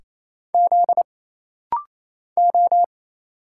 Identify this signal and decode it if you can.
Morse code, heard as a keyed tone, about 27 wpm, 700 Hz. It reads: Z O
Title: Single Letters - Rapid-Fire 28wpm